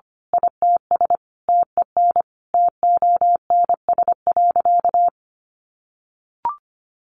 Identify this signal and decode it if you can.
Morse code, heard as a keyed tone, about 25 wpm, 700 Hz. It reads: ITS TEN TONS.
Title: A Sound Of Thunder 25wpm